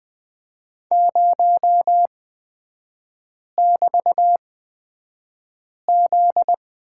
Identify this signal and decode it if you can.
Morse code, heard as a keyed tone, about 20 wpm, spaced as for 5 wpm, 700 Hz. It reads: 0<BT>Z
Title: 250 Lesson 40e - BK - Three Characters 20wpm